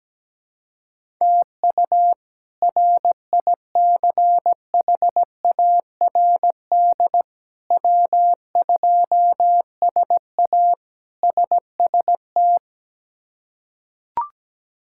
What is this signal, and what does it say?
Morse code, heard as a keyed tone, about 17 wpm, 700 Hz. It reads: TU RICHARD W2SA SST
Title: K1USN SST QSO Practice - 17wpm